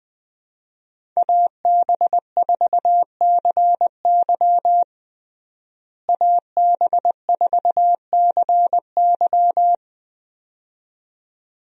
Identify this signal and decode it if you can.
Morse code, heard as a keyed tone, about 20 wpm, 700 Hz. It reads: AB4CY AB4CY
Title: USA Call Signs - 5 Characters - Repeated 2x - 20wpm